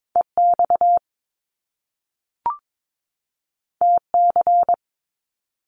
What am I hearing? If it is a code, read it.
Morse code, heard as a keyed tone, about 22 wpm, 700 Hz. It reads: EX TC
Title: Top Two Letter Combinations in Words 22wpm